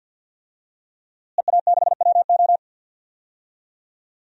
Morse code, heard as a keyed tone, about 50 wpm, 700 Hz. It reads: EA6WO
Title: DXCC-100 Call Signs - 5 Characters - Repeated 3x - 50wpm